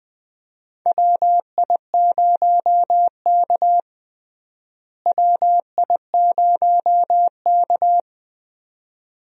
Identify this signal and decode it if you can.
Morse code, heard as a keyed tone, about 21 wpm, 700 Hz. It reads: WI0K WI0K
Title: USA Call Signs - 4 Characters - Repeated 2x - 20wpm